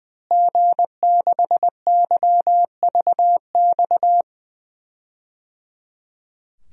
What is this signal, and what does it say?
Morse code, heard as a keyed tone, about 20 wpm, 700 Hz. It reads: G6YVX
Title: DXCC-100 Call Signs - 5 Characters - Repeated 3x - 20wpm